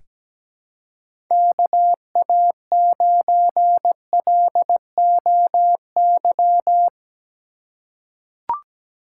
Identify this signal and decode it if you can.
Morse code, heard as a keyed tone, about 17 wpm, 700 Hz. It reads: KA9LOY
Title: USA Call Signs - 6 Characters - 4x Character Spacing - 17wpm